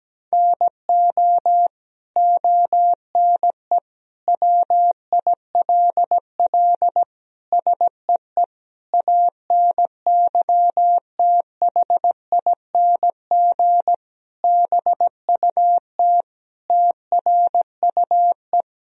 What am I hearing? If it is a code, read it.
Morse code, heard as a keyed tone, about 17 wpm, 700 Hz. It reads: NO ONE WILL SEE ANYTHING BUT TRUE
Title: Sentences from Top 1000 Words - Review Part 3 - 17wpm